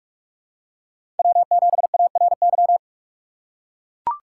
Morse code, heard as a keyed tone, about 41 wpm, 700 Hz. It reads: W7ARY
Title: USA Call Signs - 5 Characters - 6x Character Spacing - 45wpm